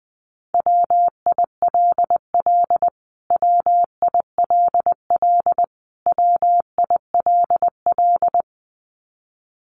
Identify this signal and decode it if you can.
Morse code, heard as a keyed tone, about 20 wpm, 700 Hz. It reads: WILL WILL WILL
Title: Top 200 Words Repeated 20wpm